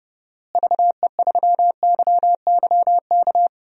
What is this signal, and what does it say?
Morse code, heard as a keyed tone, about 31 wpm, 700 Hz. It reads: VE3YYK
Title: DXCC-100 Call Signs - 6 Characters - Repeated 3x - 30wpm